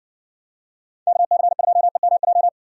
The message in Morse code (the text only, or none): KC2EMJ